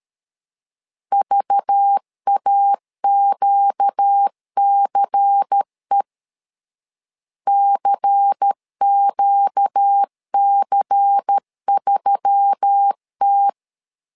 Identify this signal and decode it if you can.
Morse code, heard as a keyed tone, about 13 wpm, 800 Hz. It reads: VAQCE CQC3T
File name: LEZIONI / 5 LBVC / 5 LBVC 12wpm 1955